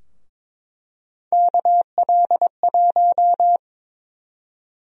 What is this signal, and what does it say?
Morse code, heard as a keyed tone, about 22 wpm, 700 Hz. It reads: KL1
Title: USA Call Sign Prefixes - Repeated 2x - 22wpm